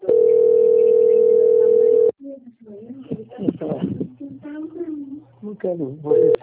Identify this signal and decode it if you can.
Ringback tone (North America), 440 + 480 Hz, with speech.
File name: out- - - - - .33421